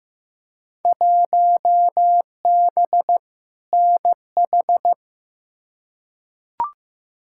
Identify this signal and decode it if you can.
Morse code, heard as a keyed tone, about 15 wpm, 700 Hz. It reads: 1B NH
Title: Field Day Exchanges Encore - 15wpm